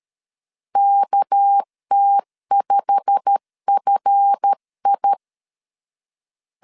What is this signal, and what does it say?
Morse code, heard as a keyed tone, about 13 wpm, 800 Hz. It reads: KT5FI
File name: LEZIONI / 8 YJ45 / 8 YJ45 12wpm 2151